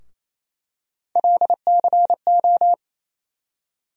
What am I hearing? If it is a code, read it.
Morse code, heard as a keyed tone, about 28 wpm, 700 Hz. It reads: LCO
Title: USA Call Sign Postfixes - 3 Letters 28wpm